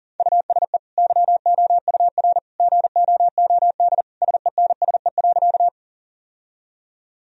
Morse code, heard as a keyed tone, about 40 wpm, 700 Hz. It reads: USE YOUR GOOD SENSE.
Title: Sentences from Top 1000 Words - Review Part 4 - 40wpm